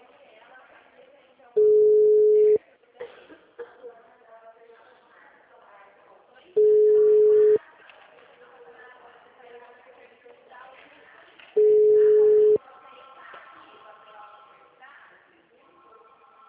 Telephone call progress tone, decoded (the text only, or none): ringback tone